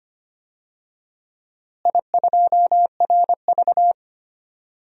Morse code, heard as a keyed tone, about 25 wpm, 700 Hz. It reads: I2RV
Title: DXCC-100 Call Signs - 4 Characters - Repeated 2x - 25wpm